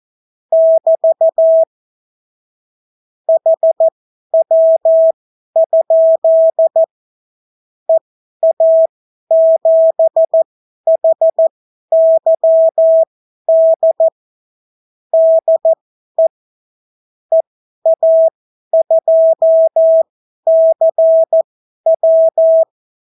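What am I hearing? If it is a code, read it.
Morse code, heard as a keyed tone, about 14 wpm, spaced as for 11 wpm, 650 Hz. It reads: <BT> HW? EA7HYD DE EA2CW
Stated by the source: Example Spanish CW QSO from a Morse learning book, 14 wpm characters with 8 wpm spacing.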